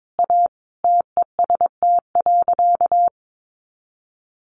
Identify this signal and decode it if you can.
Morse code, heard as a keyed tone, about 22 wpm, 700 Hz. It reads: A TEST.
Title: Additional Easy Phrases 22wpm